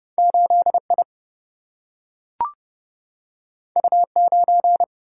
Morse code, heard as a keyed tone, about 30 wpm, 700 Hz. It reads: 8 I U 9
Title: Single-Letter Single-Number 30wpm